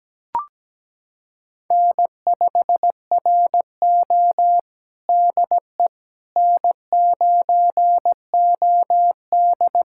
Morse code, heard as a keyed tone, about 17 wpm, 700 Hz. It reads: N5RO DE N9OD
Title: USA Callsign DE Callsign - 4 Characters - 17wpm